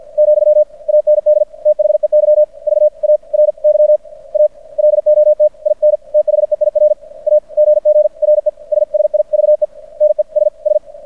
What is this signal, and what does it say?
Morse code, heard as a keyed tone, about 50 wpm, spaced as for 41 wpm, 600 Hz. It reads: , AND THEY VARY A LOT IN THEIR AGGRESSIVENESS
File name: archived / CRSnet